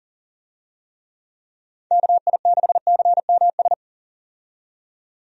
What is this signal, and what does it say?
Morse code, heard as a keyed tone, about 37 wpm, 700 Hz. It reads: KI6CMS
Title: USA Call Signs - 6 Characters - 40wpm